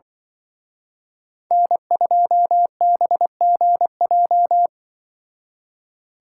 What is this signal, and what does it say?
Morse code, heard as a keyed tone, about 24 wpm, 700 Hz. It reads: N2BGJ
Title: Speed Racing - USA 1x3 Call Signs 20wpm